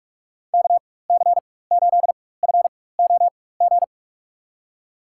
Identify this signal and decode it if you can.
Morse code, heard as a keyed tone, about 43 wpm, spaced as for 21 wpm, 700 Hz. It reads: KC8FOG
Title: USA Call Signs - 4x Character Spacing - 45wpm